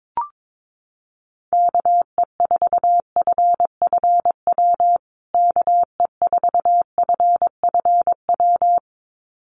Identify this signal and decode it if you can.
Morse code, heard as a keyed tone, about 22 wpm, 700 Hz. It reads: KE4FFW KE4FFW
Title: USA Call Signs Repeated 2x 22wpm